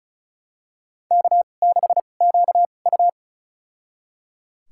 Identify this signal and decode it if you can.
Morse code, heard as a keyed tone, about 36 wpm, spaced as for 25 wpm, 700 Hz. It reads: K6QU
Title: USA Call Signs - 4 Characters - 2x Character Spacing - 35wpm